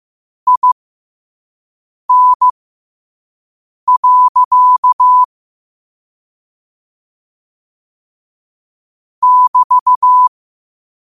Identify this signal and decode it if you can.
Morse code, heard as a keyed tone, about 15 wpm, spaced as for 5 wpm, 1000 Hz. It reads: IN. <BT>
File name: qotdfiles / 5wpm